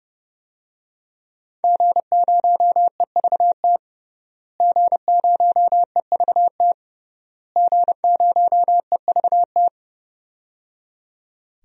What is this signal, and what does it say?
Morse code, heard as a keyed tone, about 30 wpm, 700 Hz. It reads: G0EVT G0EVT G0EVT
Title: DXCC-100 Call Signs - 5 Characters - Repeated 3x - 30wpm